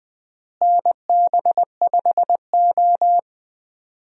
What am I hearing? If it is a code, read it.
Morse code, heard as a keyed tone, about 20 wpm, 700 Hz. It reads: NB5O
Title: USA Call Signs - 4 Characters - Repeated 3x - 20wpm